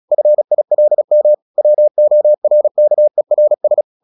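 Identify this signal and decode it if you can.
Morse code, heard as a keyed tone, about 36 wpm, 600 Hz. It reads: FILM WORKERS
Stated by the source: Morse practice at 36 wpm, text of Ken Follett's novel Never, from CW med Gnister.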